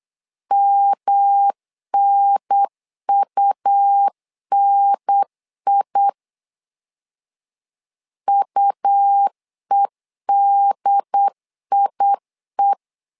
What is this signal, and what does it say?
Morse code, heard as a keyed tone, about 8 wpm, 800 Hz. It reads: MNUNI UEDIE
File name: LEZIONI / 3 MWRO / 3 MWR 8wpm 2444